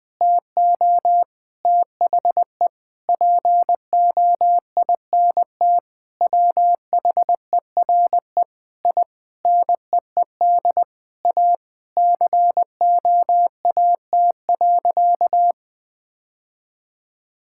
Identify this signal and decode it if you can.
Morse code, heard as a keyed tone, about 20 wpm, 700 Hz. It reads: TO THE POINT WHERE I NEED A COAT.
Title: Sentences with 4-Word Phrases - 20wpm